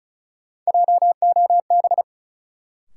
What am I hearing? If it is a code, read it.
Morse code, heard as a keyed tone, about 36 wpm, 700 Hz. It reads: JOB
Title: Sentences from Top 600 Words - New Word Drill - 35wpm